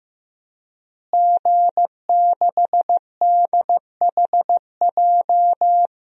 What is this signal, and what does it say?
Morse code, heard as a keyed tone, about 15 wpm, 700 Hz. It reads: G6DHJ
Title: DXCC-100 Call Signs - 5 Characters - Repeated 3x - 15wpm